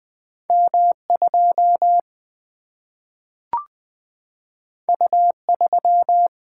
Morse code, heard as a keyed tone, about 20 wpm, 700 Hz. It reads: M 2 U 3
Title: Single-Letter Single-Number 20wpm